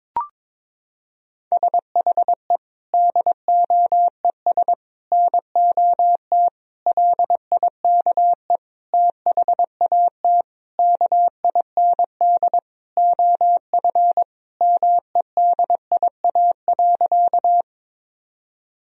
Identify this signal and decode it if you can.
Morse code, heard as a keyed tone, about 22 wpm, 700 Hz. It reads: SHE DOES NOT LIKE THAT KIND OF MEDIA.
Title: Sentences from Top 1000 Words - Review Part 4 - 22wpm